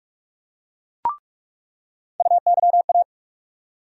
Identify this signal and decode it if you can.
Morse code, heard as a keyed tone, about 44 wpm, 700 Hz. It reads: UYA
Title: USA Call Sign Postfixes - 3 Letters 45wpm